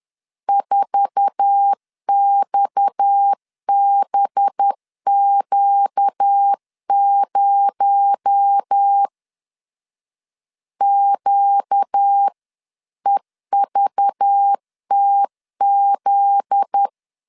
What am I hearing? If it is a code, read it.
Morse code, heard as a keyed tone, about 10 wpm, 800 Hz. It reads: 4XBQ0 QEVTZ
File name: LEZIONI / 8 YJ45 / 8 YJ4 10wpm 2633